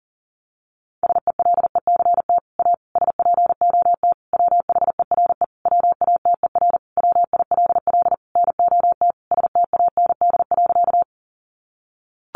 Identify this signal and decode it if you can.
Morse code, heard as a keyed tone, about 40 wpm, 700 Hz. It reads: SELECT A SPOT WHERE WATER WILL NOT STAND.
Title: Sentences from Top 1300 Words 40wpm